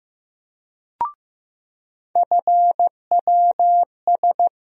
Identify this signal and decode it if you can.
Morse code, heard as a keyed tone, about 15 wpm, 700 Hz. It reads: FWS
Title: USA Call Sign Postfixes - 3 Letters 15wpm